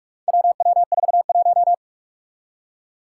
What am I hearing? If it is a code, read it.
Morse code, heard as a keyed tone, about 46 wpm, 700 Hz. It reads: WW41
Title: Four Challenging Letter-Number Combinations 45wpm